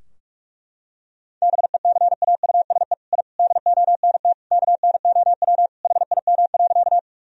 Morse code, heard as a keyed tone, about 46 wpm, 700 Hz. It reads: BECAUSE I DONT KNOW HIM.
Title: Sentences from Top 1000 Words - Review Part 3 - 45wpm